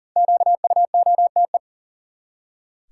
Morse code, heard as a keyed tone, about 40 wpm, 700 Hz. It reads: QUOTE
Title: Five Letter Words - 40wpm